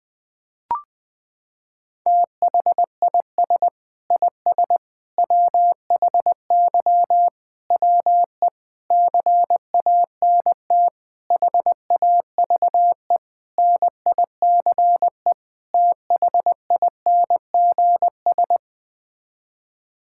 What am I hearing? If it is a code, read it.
Morse code, heard as a keyed tone, about 20 wpm, 700 Hz. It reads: THIS IS WHY WE CANT HAVE NICE THINGS
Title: Sets of 8 Words - 20wpm